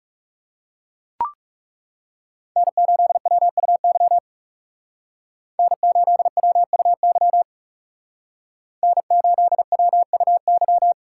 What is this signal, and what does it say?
Morse code, heard as a keyed tone, about 36 wpm, 700 Hz. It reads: N8WUY N8WUY N8WUY
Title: Speed Racing - USA Call Signs 30wpm